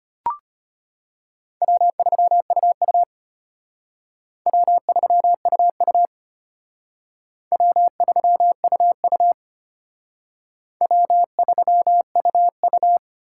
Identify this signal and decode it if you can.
Morse code, heard as a keyed tone, about 31 wpm, 700 Hz. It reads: W3UU W3UU W3UU W3UU
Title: Speed Racing - USA 1x2 Call Signs 25wpm